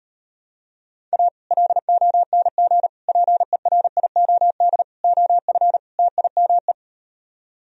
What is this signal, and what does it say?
Morse code, heard as a keyed tone, about 38 wpm, 700 Hz. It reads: A LONG PERIOD OF TIME